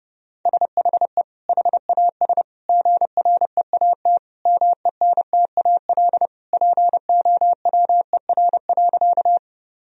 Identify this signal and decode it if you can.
Morse code, heard as a keyed tone, about 28 wpm, 700 Hz. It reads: SHE HAS GREAT MENTAL POWER.
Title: Sentences from Top 1000 Words - Review Part 4 - 30wpm